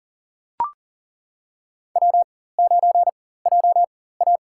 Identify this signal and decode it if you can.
Morse code, heard as a keyed tone, about 37 wpm, spaced as for 18 wpm, 700 Hz. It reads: W9JA
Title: USA Call Signs - 4 Characters - 4x Character Spacing - 40wpm